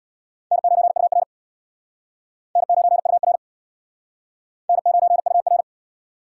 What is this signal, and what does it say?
Morse code, heard as a keyed tone, about 63 wpm, 700 Hz. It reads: N9UR N9UR N9UR